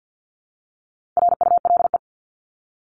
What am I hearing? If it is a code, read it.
Morse code, heard as a keyed tone, about 50 wpm, 700 Hz. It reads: RULE
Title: ES:S Suffix - Top 100 Words with Base Words 50wpm